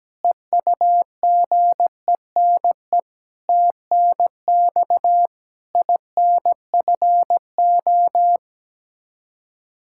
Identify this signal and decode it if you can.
Morse code, heard as a keyed tone, about 17 wpm, 700 Hz. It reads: EUGENE TNX INFO
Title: Practice QSO - Exchange 3 - More Info - 17wpm